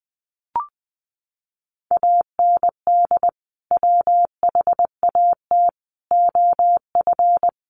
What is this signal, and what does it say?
Morse code, heard as a keyed tone, about 20 wpm, 700 Hz. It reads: AND WHAT OF
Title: Sentences from Top 1400 Words 20wpm